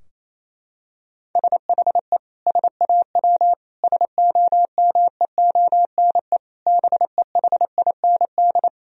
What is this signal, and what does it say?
Morse code, heard as a keyed tone, about 28 wpm, 700 Hz. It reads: SHE SAW SOMEONE BEHIND
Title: Sentences from Top 1000 Words - Review Part 4 - 28wpm